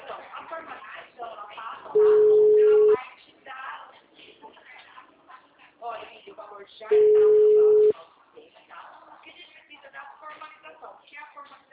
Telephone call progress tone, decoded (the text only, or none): ringback tone